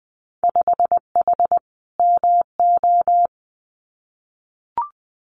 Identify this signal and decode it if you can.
Morse code, heard as a keyed tone, about 20 wpm, 700 Hz. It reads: 5H MO
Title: Winter Field Day Exchanges Encore 20wpm